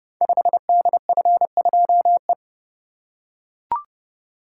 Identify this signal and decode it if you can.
Morse code, heard as a keyed tone, about 30 wpm, 700 Hz. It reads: 5DF2E
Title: Code Groups - 5 Letter:Numbers - 30wpm